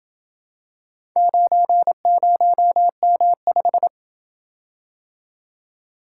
Morse code, heard as a keyed tone, about 27 wpm, 700 Hz. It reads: 90M5